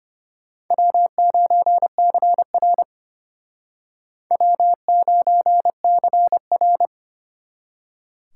Morse code, heard as a keyed tone, about 34 wpm, 700 Hz. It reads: W9CR W9CR